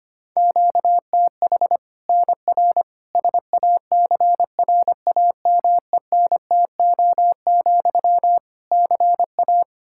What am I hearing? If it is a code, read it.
Morse code, heard as a keyed tone, about 25 wpm, 700 Hz. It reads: QTH NR SACRAMENTO, CA
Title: Practice QSO - Exchange 1 - Beginning - 25wpm